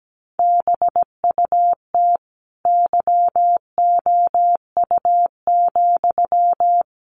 Morse code, heard as a keyed tone, about 17 wpm, 700 Hz. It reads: BUT YOU,
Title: Sentences from Top 1000 Words 17wpm